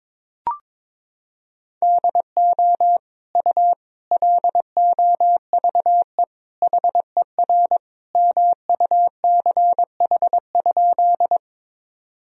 Morse code, heard as a keyed tone, about 22 wpm, 700 Hz. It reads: DO U LOVE HER MUCH?
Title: Sentences from Top 400 Words and Abbreviations 22wpm